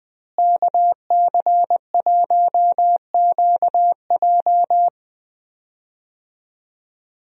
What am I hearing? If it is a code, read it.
Morse code, heard as a keyed tone, about 21 wpm, 700 Hz. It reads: KC1QJ
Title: USA Call Signs - 5 Characters - Repeated 2x - 20wpm